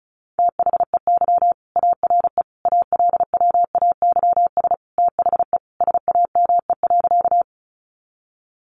Morse code, heard as a keyed tone, about 34 wpm, 700 Hz. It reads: THEY ARE ALWAYS THE SAME.
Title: Sentences from Top 300 Words 35wpm